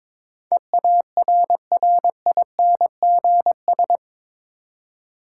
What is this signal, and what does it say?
Morse code, heard as a keyed tone, about 22 wpm, 700 Hz. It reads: EARRINGS
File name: Difficult Words with Hints - 22wpm